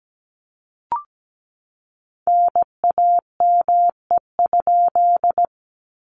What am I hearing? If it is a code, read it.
Morse code, heard as a keyed tone, about 17 wpm, 700 Hz. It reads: NAME?
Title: Common QSO Phrases Encore 17wpm